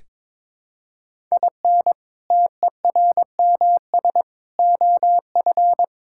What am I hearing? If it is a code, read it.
Morse code, heard as a keyed tone, about 22 wpm, 700 Hz. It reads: IN TERMS OF
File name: practice / 22wpm / Sentences with 4-Word Phrases - 22wpm